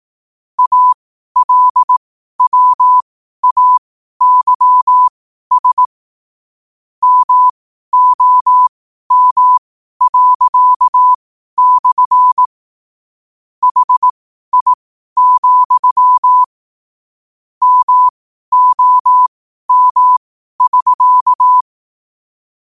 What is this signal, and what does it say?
Morse code, heard as a keyed tone, about 18 wpm, spaced as for 12 wpm, 1000 Hz. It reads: ALWAYS MOM./ HI, MOM<SK>
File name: qotdfiles / 12wpm